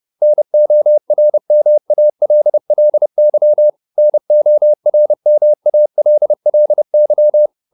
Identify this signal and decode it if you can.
Morse code, heard as a keyed tone, about 30 wpm, 600 Hz. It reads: NORMALLY NORMALLY